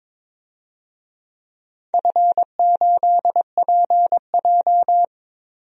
Morse code, heard as a keyed tone, about 21 wpm, 700 Hz. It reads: F8PJ
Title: DXCC-100 Call Signs - 4 Characters - Repeated 2x - 22wpm